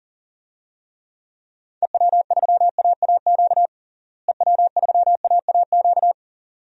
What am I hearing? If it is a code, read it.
Morse code, heard as a keyed tone, about 37 wpm, 700 Hz. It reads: EW3AAQ EW3AAQ
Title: DXCC-100 Call Signs - 6 Characters - Repeated 2x - 40wpm